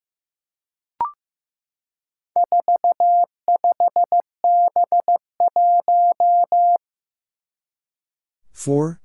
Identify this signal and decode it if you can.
Morse code, heard as a keyed tone, about 15 wpm, 700 Hz. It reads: 45B1
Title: Four Challenging Letter-Number Combinations 15wpm